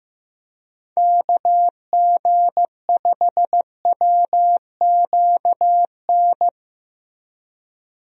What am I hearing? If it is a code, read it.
Morse code, heard as a keyed tone, about 16 wpm, 700 Hz. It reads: KG5WQN
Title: USA Call Signs - 6 Characters - Repeated 3x - 15wpm